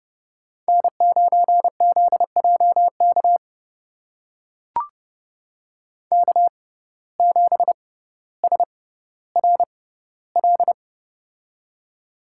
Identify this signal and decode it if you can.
Morse code, heard as a keyed tone, about 30 wpm, spaced as for 15 wpm, 700 Hz. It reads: N9ZJK K7SRL
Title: USA Call Signs - 5 Characters - 6x Character Spacing - 30wpm